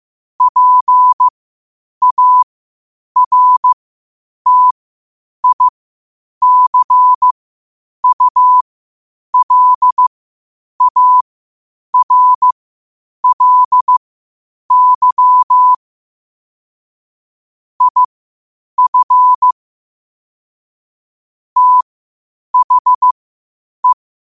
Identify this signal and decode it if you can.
Morse code, heard as a keyed tone, about 15 wpm, spaced as for 8 wpm, 1000 Hz. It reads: PARTICULARLY IF THE
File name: qotdfiles / 8wpm